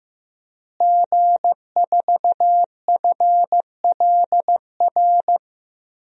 Morse code, heard as a keyed tone, about 15 wpm, 700 Hz. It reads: G4FLR
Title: DXCC-100 Call Signs - 5 Characters - Repeated 3x - 15wpm